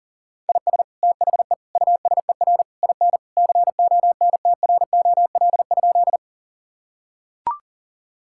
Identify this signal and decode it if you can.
Morse code, heard as a keyed tone, about 40 wpm, 700 Hz. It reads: IS THE USER IN CONTROL?
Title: Sentences from Top 1000 Words - Review Part 5 - 40wpm